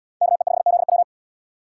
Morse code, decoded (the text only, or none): KE5ZW